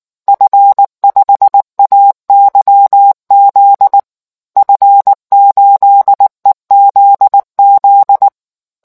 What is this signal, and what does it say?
Morse code, heard as a keyed tone, about 19 wpm, 800 Hz. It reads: F5AYZ F8EZZ